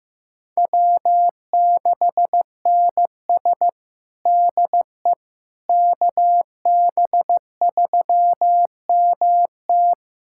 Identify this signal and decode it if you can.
Morse code, heard as a keyed tone, about 15 wpm, 700 Hz. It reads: W6NS DE KB3MT
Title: Practice QSO - Exchange 4 - Ending - 15wpm